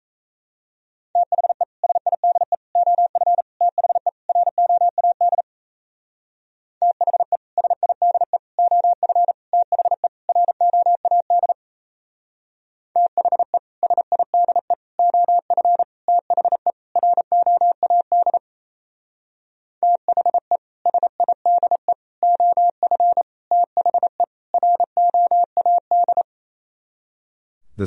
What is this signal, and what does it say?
Morse code, heard as a keyed tone, about 34 wpm, 700 Hz. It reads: THE SIDE OF THE ROAD THE SIDE OF THE ROAD THE SIDE OF THE ROAD THE SIDE OF THE ROAD
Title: Speed-Racing - Sets of 5 Words 28wpm